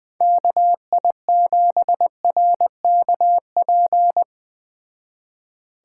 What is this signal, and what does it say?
Morse code, heard as a keyed tone, about 20 wpm, 700 Hz. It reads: KI7RKP